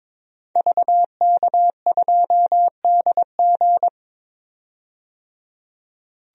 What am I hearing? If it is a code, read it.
Morse code, heard as a keyed tone, about 22 wpm, 700 Hz. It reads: VK2DG
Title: DXCC-100 Call Signs - 5 Characters - 22wpm